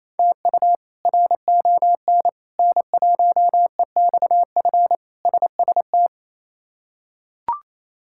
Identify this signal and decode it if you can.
Morse code, heard as a keyed tone, about 28 wpm, 700 Hz. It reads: TU RON N1EXF SST
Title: K1USN SST QSO Practice - 28wpm